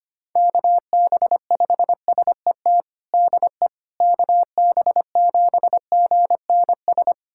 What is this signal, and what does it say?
Morse code, heard as a keyed tone, about 25 wpm, 700 Hz. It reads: KB5SET DE KB7GNS
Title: USA Callsign DE Callsign - 6 Characters - 25wpm